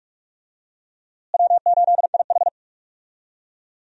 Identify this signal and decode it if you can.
Morse code, heard as a keyed tone, about 45 wpm, 700 Hz. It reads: W8IH
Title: USA Call Signs - 4 Characters - Repeated 2x - 45wpm